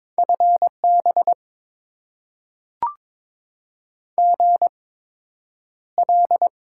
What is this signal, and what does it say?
Morse code, heard as a keyed tone, about 22 wpm, spaced as for 9 wpm, 700 Hz. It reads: FB GL
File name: Top 90 QSO Elements - 8x Character Spacing - 22wpm